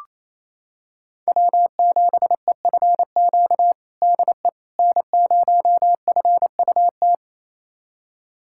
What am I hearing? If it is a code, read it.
Morse code, heard as a keyed tone, about 28 wpm, 700 Hz. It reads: W7EFQ DE N0FUT
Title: USA Callsign DE Callsign - 5 Characters - 28wpm